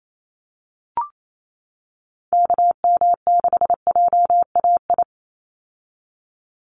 Morse code, heard as a keyed tone, about 28 wpm, 700 Hz. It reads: KM6JAI
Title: USA Call Signs 28wpm